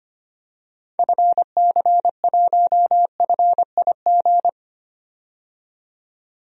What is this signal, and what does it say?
Morse code, heard as a keyed tone, about 25 wpm, 700 Hz. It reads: FC1FIG
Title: DXCC-100 Call Signs - 6 Characters - 25wpm